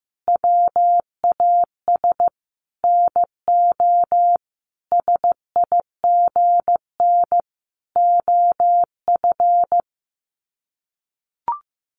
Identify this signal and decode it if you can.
Morse code, heard as a keyed tone, about 15 wpm, 700 Hz. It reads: WAS NO SIGN OF
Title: Sets of 5 Words Encore2 15wpm